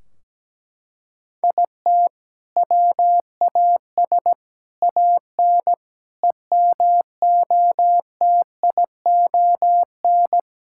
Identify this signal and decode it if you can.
Morse code, heard as a keyed tone, about 17 wpm, 700 Hz. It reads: IT WAS AN EMOTION
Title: Sentences from Top 1600 Words 17wpm